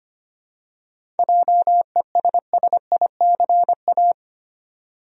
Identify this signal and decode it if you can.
Morse code, heard as a keyed tone, about 25 wpm, 700 Hz. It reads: JESSICA
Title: USA Common Names - 25wpm